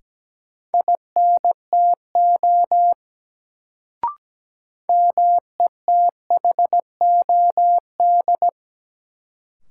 Morse code, heard as a keyed tone, about 17 wpm, 700 Hz. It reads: INTO METHOD
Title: Sentences from Top 600 Words - New Word Drill - 17wpm